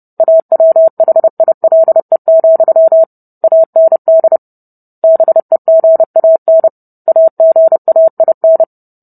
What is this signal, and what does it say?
Morse code, heard as a keyed tone, about 30 wpm, 650 Hz. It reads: AWHILE, AND BEGAN AGAIN